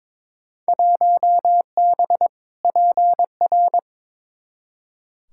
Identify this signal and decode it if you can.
Morse code, heard as a keyed tone, about 21 wpm, 700 Hz. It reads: 1B PR
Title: Field Day Exchanges Encore - 22wpm